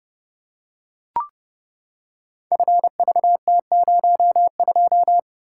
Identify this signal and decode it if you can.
Morse code, heard as a keyed tone, about 30 wpm, 700 Hz. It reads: FVT02
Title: Code Groups - 5 Letter:Numbers - 30wpm